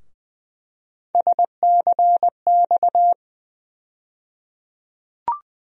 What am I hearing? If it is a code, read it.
Morse code, heard as a keyed tone, about 20 wpm, 700 Hz. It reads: SCX
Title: USA Call Sign Postfixes - 3 Letters 20wpm